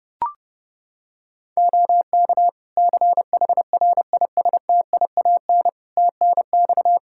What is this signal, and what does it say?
Morse code, heard as a keyed tone, about 30 wpm, 700 Hz. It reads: OK CHRISTIAN TNX
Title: Practice QSO - Exchange 4 - Ending - 30wpm